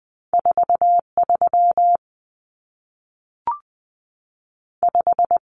Morse code, heard as a keyed tone, about 20 wpm, 700 Hz. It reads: 43 5
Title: Common Cut Numbers - 2-Digit 20wpm